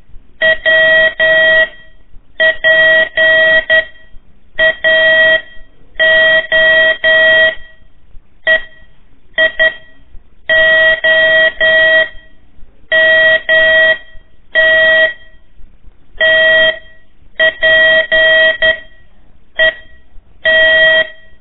Morse code, heard as a keyed tone, about 11 wpm, 1900 Hz. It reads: W P A O E I O M T T P E T
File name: 3 lezione CW 1 AIR